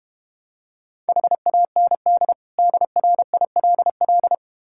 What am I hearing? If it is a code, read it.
Morse code, heard as a keyed tone, about 31 wpm, 700 Hz. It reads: HAND DRILL
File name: Easy Words - Speed-Racing with Hints - 25wpm